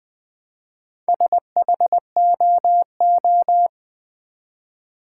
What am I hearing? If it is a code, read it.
Morse code, heard as a keyed tone, about 20 wpm, 700 Hz. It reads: SHOO
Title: Build A Word - 1 Letter At A Time - 5-Letter Words - 20wpm